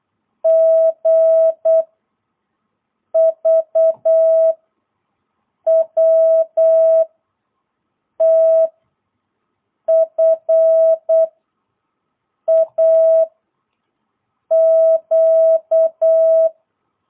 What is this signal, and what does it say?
Morse code, heard as a keyed tone, about 8 wpm, 660 Hz. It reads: G V W T F A Q
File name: jh8cbh / 2022cw